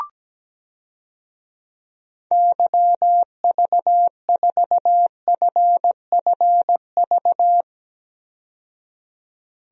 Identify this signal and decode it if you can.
Morse code, heard as a keyed tone, about 17 wpm, 700 Hz. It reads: YV4FFV
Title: DXCC-100 Call Signs - 6 Characters - 17wpm